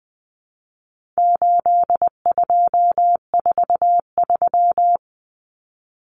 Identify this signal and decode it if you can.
Morse code, heard as a keyed tone, about 20 wpm, 700 Hz. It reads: 8243
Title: Common Cut Numbers - 4-Digit 20wpm